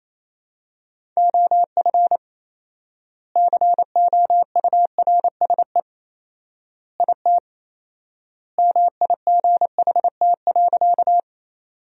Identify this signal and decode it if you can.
Morse code, heard as a keyed tone, about 28 wpm, 700 Hz. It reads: OF COURSE IT MIGHT.
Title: 4x Word-Spacing - Sentences from Top 500 Words - 28wpm